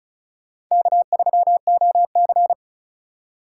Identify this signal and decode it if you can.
Morse code, heard as a keyed tone, about 43 wpm, 700 Hz. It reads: K3OC
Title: Speed Racing - USA 1x2 Call Signs 35wpm